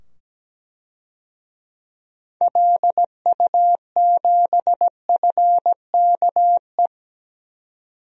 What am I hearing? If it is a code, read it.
Morse code, heard as a keyed tone, about 17 wpm, 700 Hz. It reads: LU7FKE
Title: DXCC-100 Call Signs - 6 Characters - 17wpm